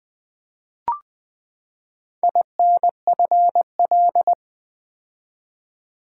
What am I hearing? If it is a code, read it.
Morse code, heard as a keyed tone, about 20 wpm, 700 Hz. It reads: INFL
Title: Build A Word - 4 Letters At A Time - 11-Letter Words - 20wpm